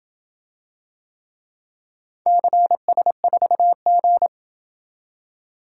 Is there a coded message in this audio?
Yes: Morse code from 2.3 to 4.3 s, heard as a keyed tone, about 26 wpm, 700 Hz. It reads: CS4G